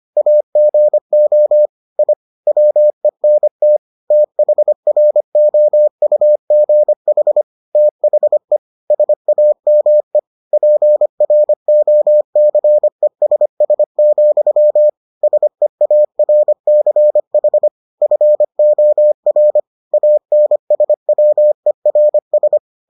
Morse code, heard as a keyed tone, about 25 wpm, 600 Hz. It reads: AGO I WENT THROUGH THE SAME PROCESS, SEARCH FOR ANSWERS